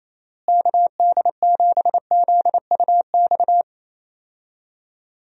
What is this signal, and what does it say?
Morse code, heard as a keyed tone, about 28 wpm, 700 Hz. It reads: KD7ZUX
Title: USA Call Signs - 6 Characters - Repeated 2x - 28wpm